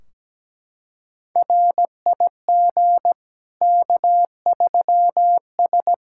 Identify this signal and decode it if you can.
Morse code, heard as a keyed tone, about 17 wpm, 700 Hz. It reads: RIG K3S
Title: K1USN SST QSO Practice 17wpm